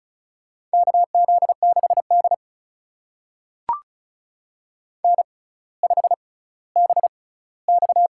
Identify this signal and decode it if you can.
Morse code, heard as a keyed tone, about 36 wpm, spaced as for 18 wpm, 700 Hz. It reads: KZ6D N5BX
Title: USA Call Signs - 4 Characters - 6x Character Spacing - 35wpm